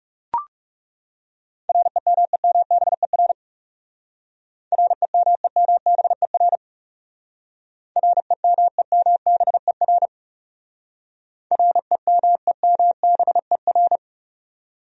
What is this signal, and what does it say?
Morse code, heard as a keyed tone, about 37 wpm, 700 Hz. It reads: REMEMBER REMEMBER REMEMBER REMEMBER
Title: Speed-Racing Top 500 Words 30wpm